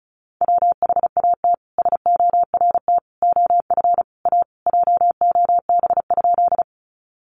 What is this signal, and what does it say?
Morse code, heard as a keyed tone, about 34 wpm, 700 Hz. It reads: WHAT SORT OF A JOB?
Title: Sentences from Top 1300 Words 35wpm